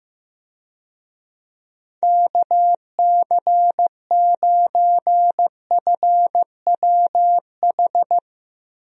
Morse code, heard as a keyed tone, about 15 wpm, 700 Hz. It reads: KC9FWH
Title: USA Call Signs - 6 Characters - Repeated 2x - 15wpm